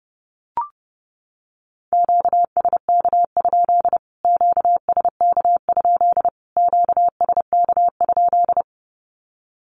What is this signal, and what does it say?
Morse code, heard as a keyed tone, about 29 wpm, 700 Hz. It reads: QSK? QSK? QSK?
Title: Q-Codes Repeated 3x 30wpm